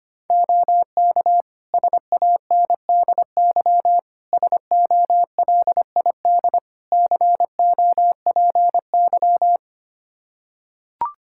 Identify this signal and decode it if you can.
Morse code, heard as a keyed tone, about 25 wpm, 700 Hz. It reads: OK SANDY SOLID COPY
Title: Practice QSO - Exchange 3 - More Info - 25wpm